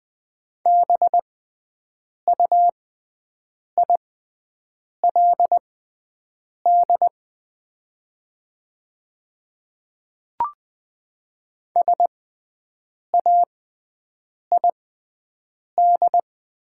Morse code, heard as a keyed tone, about 20 wpm, spaced as for 7 wpm, 700 Hz. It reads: BUILD SAID
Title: 6x Character Spacing - Top 200 Words - 20wpm